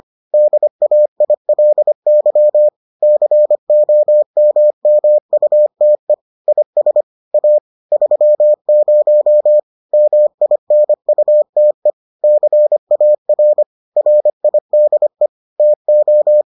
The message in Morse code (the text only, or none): DAILY COMMUTE IS A 30 MINUTE CAR RIDE TO